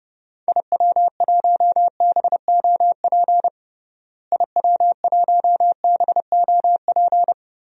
Morse code, heard as a keyed tone, about 32 wpm, 700 Hz. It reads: IW1BOP IW1BOP
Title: DXCC-100 Call Signs - 6 Characters - Repeated 2x - 30wpm